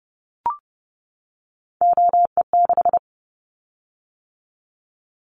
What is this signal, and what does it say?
Morse code, heard as a keyed tone, about 30 wpm, 700 Hz. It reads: OE6
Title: Callsign Prefixes for Top 100 DXCC - 30wpm